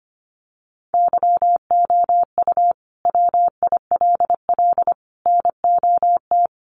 Morse code, heard as a keyed tone, about 25 wpm, 700 Hz. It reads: YOU WILL NOT
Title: Sentences from Top 1100 Words 25wpm